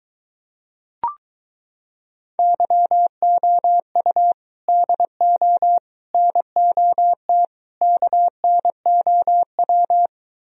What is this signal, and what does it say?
Morse code, heard as a keyed tone, about 23 wpm, 700 Hz. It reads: YOU DO NOT KNOW